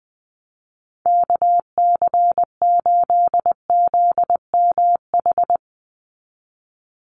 Morse code, heard as a keyed tone, about 20 wpm, 700 Hz. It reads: KC8ZMH
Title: USA Call Signs 20wpm